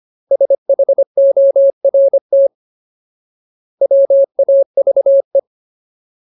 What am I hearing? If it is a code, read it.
Morse code, heard as a keyed tone, about 25 wpm, 550 Hz. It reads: SHORT WAVE
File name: Ultimate Word List / 25WPM Ultimate